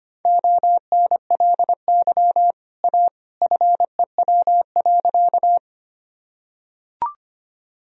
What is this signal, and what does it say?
Morse code, heard as a keyed tone, about 25 wpm, 700 Hz. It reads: ONLY A FEW.
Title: Sentences from Top 1000 Words - Review Part 1 - 25wpm